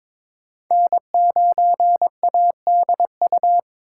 Morse code, heard as a keyed tone, about 23 wpm, 700 Hz. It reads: N9ADU
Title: USA Call Signs - 5 Characters - 4x Character Spacing - 22wpm